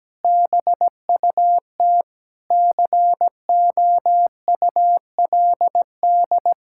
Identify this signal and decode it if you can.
Morse code, heard as a keyed tone, about 17 wpm, 700 Hz. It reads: BUT COULD
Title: Sentences from Top 1000 Words - Review Part 5 - 17wpm